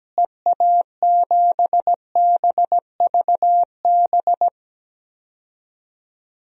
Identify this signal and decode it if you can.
Morse code, heard as a keyed tone, about 17 wpm, 700 Hz. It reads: EA7BVB
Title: DXCC-100 Call Signs - 6 Characters - Repeated 2x - 17wpm